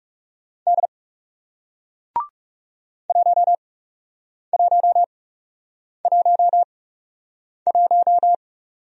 Morse code, heard as a keyed tone, about 38 wpm, spaced as for 8 wpm, 700 Hz. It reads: D 1111
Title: Speed-Racing Single Letter-Number 30wpm